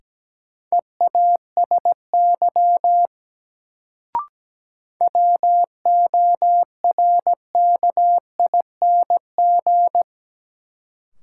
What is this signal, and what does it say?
Morse code, heard as a keyed tone, about 17 wpm, 700 Hz. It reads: EASY WORKING
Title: Sentences from Top 700 Words - New Word Drill - 17wpm